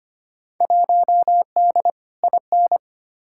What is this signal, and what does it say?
Morse code, heard as a keyed tone, about 25 wpm, 700 Hz. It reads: 1D IN
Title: Field Day Exchanges - 25wpm